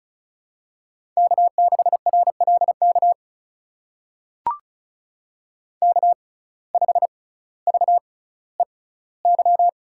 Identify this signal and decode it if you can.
Morse code, heard as a keyed tone, about 36 wpm, spaced as for 18 wpm, 700 Hz. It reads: K6RLK K5VEY
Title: USA Call Signs - 5 Characters - 6x Character Spacing - 35wpm